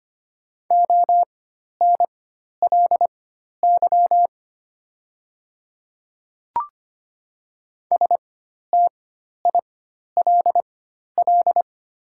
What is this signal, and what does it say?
Morse code, heard as a keyed tone, about 25 wpm, spaced as for 12 wpm, 700 Hz. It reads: ONLY STILL
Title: 4x Character Spacing - Top 300 Words - 25wpm